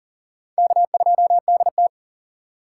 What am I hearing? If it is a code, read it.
Morse code, heard as a keyed tone, about 40 wpm, 700 Hz. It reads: K2DT
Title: USA Call Signs - 4 Characters - 4x Character Spacing - 40wpm